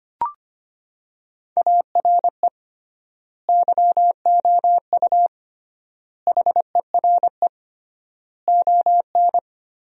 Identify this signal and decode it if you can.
Morse code, heard as a keyed tone, about 25 wpm, 700 Hz. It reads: ARE YOU HERE ON
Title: 3x Word-Spacing - Sentences from Top 400 Words - 25wpm